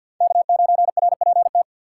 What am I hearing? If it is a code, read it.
Morse code, heard as a keyed tone, about 53 wpm, 700 Hz. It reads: K9RPT
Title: USA Call Signs - 5 Characters - 6x Character Spacing - 50wpm